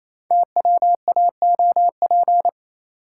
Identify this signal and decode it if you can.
Morse code, heard as a keyed tone, about 27 wpm, 700 Hz. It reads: TWAOP
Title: Code Groups - 5 Letters - 28wpm